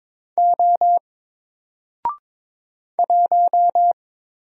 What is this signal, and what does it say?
Morse code, heard as a keyed tone, about 22 wpm, 700 Hz. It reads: O 1
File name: Single-Letter-Number-22wpm